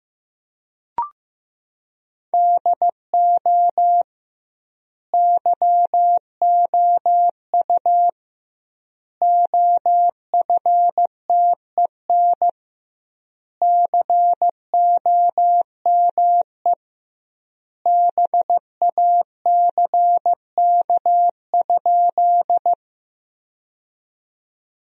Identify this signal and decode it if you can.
Morse code, heard as a keyed tone, about 15 wpm, 700 Hz. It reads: DO YOU OFTEN COME BACK?
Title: 2x Word-Spacing - Sentences from Top 300 Words - 15wpm